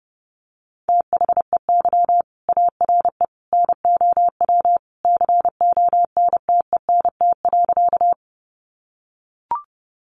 Morse code, heard as a keyed tone, about 29 wpm, 700 Hz. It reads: THEY ARE NOW CONTENT.
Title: Sentences from Top 900 Words 30wpm